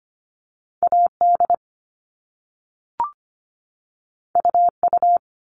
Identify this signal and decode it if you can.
Morse code, heard as a keyed tone, about 25 wpm, 700 Hz. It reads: AD UU
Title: All Cut Numbers - 2-Digit 25wpm